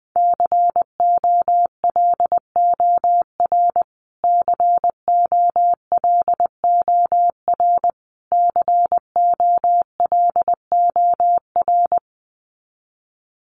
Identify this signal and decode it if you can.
Morse code, heard as a keyed tone, about 20 wpm, 700 Hz. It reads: COLOR COLOR COLOR
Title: Top 500 Words Repeated 20wpm